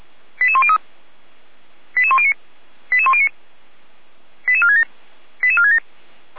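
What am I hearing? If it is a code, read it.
Five-tone selective call (ZVEI), heard as quick runs of five tones, tones of about 70 ms. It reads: 88103 88109 88109 88467 88467
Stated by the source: ZVEI five-tone selective call received by SDR, feuerwehr-orlamuende fire brigade (f85360).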